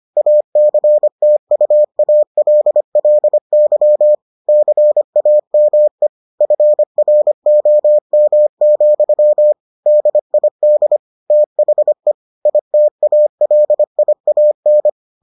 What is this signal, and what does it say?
Morse code, heard as a keyed tone, about 25 wpm, 600 Hz. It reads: ACTUALLY CAME FROM, DID THE ITALIAN